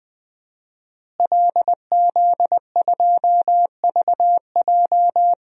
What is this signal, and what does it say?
Morse code, heard as a keyed tone, about 20 wpm, 700 Hz. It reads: LZ2VJ
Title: DXCC-100 Call Signs - 5 Characters - Repeated 2x - 20wpm